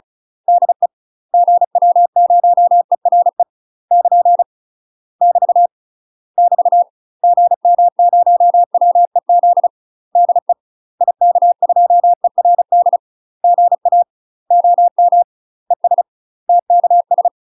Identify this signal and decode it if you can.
Morse code, heard as a keyed tone, about 34 wpm, 700 Hz. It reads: DE GW0ERE <KN> <BT> <BT> GM0WEZ DE IK2ERD GA OM ES TKS
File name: UK MASTER QSOs TRAINING / 35WPM / UK MASTER QSOs 0004